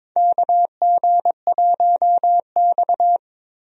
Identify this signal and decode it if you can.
Morse code, heard as a keyed tone, about 22 wpm, 700 Hz. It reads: KG1X
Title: USA Call Signs - 4 Characters - Repeated 2x - 22wpm